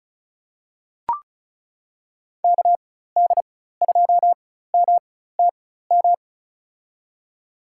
Morse code, heard as a keyed tone, about 36 wpm, spaced as for 16 wpm, 700 Hz. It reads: KD2MTM
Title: USA Call Signs - 6 Characters - 4x Character Spacing - 35wpm